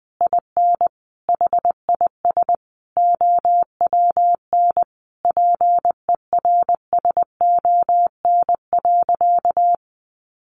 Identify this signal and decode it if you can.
Morse code, heard as a keyed tone, about 20 wpm, 700 Hz. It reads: IN HIS OWN PERSON.
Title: Sentences from Top 600 Words 20wpm